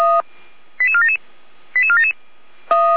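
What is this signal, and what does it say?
Five-tone selective call (ZVEI), heard as quick runs of five tones, tones of about 70 ms. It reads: 88477 88477
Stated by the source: ZVEI five-tone selective call received via SDR, from the Feuerwehr Orlamünde fire brigade audio archive.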